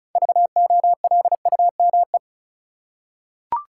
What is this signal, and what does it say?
Morse code, heard as a keyed tone, about 34 wpm, 700 Hz. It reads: VOLUME
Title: Six Letter Words - 35wpm